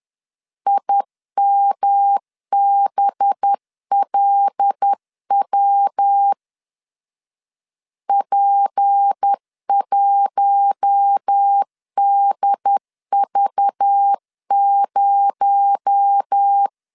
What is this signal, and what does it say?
Morse code, heard as a keyed tone, about 10 wpm, 800 Hz. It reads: IMBLW P1DV0
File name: LEZIONI / 7 GF9Z / 7 G 10wpm 2501